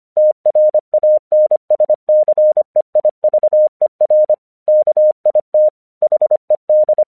Morse code, heard as a keyed tone, about 25 wpm, 610 Hz. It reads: TRANSCEIVER KIT HED
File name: 1 English 25wpm / AQC17